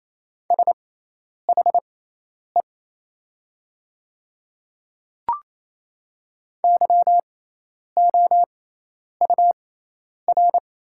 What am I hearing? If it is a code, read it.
Morse code, heard as a keyed tone, about 28 wpm, spaced as for 10 wpm, 700 Hz. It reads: SHE YOUR
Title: 6x Character Spacing - Top 100 Words - 28wpm